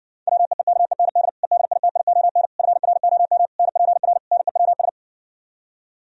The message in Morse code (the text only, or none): WEEKEND EDITION FROM NPR NEWS